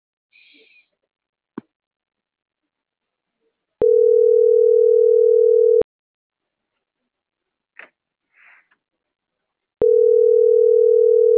Ringback tone (North America), 440 + 480 Hz.